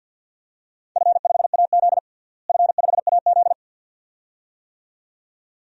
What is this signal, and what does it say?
Morse code, heard as a keyed tone, about 47 wpm, 700 Hz. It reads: F5AZ F5AZ
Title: DXCC-100 Call Signs - 4 Characters - Repeated 2x - 50wpm